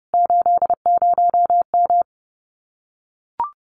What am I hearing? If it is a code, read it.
Morse code, heard as a keyed tone, about 30 wpm, 700 Hz. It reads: 80M
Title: Common QSO Phrases Encore 30wpm